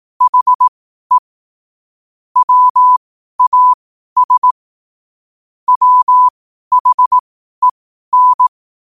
Morse code, heard as a keyed tone, about 18 wpm, spaced as for 12 wpm, 1000 Hz. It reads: HE WAS WHEN
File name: qotdfiles / 12wpm